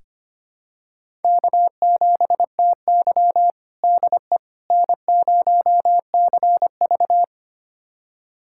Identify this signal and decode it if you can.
Morse code, heard as a keyed tone, about 25 wpm, 700 Hz. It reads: K7TY DE N0CV
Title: USA Callsign DE Callsign - 4 Characters - 25wpm